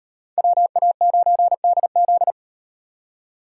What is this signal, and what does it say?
Morse code, heard as a keyed tone, about 38 wpm, 700 Hz. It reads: WA9DZ